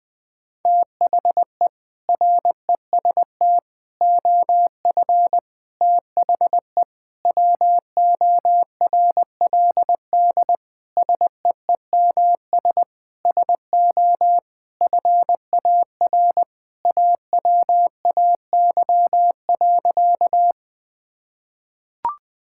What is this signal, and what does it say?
Morse code, heard as a keyed tone, about 20 wpm, 700 Hz. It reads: THE REST OF THE WORLD SEEMS SO FAR AWAY.
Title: Sentences with 5-Word Phrases - 20wpm